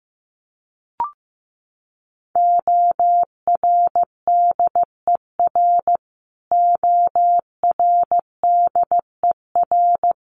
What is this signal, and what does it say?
Morse code, heard as a keyed tone, about 15 wpm, 700 Hz. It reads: ORDER ORDER
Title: Top 500 Words Repeated 15wpm